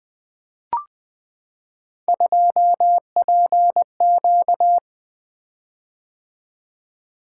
Morse code, heard as a keyed tone, about 20 wpm, 700 Hz. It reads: 2PQ
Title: Three Letter-Number Combinations - 20wpm